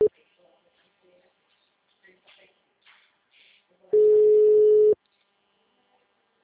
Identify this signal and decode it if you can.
Ringback tone (Europe, 425 Hz), with speech.
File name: out- -2014- - - .148